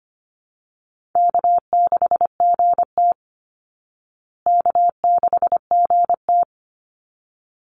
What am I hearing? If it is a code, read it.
Morse code, heard as a keyed tone, about 25 wpm, 700 Hz. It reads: K6GT K6GT
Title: Call Signs Repeated 3x - 25wpm